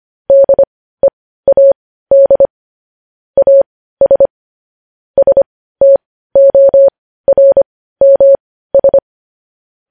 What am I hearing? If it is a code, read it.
Morse code, heard as a keyed tone, about 25 wpm, spaced as for 15 wpm, 550 Hz. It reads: DEAD AS STORMS
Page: news headlines in morse code at 15 wpm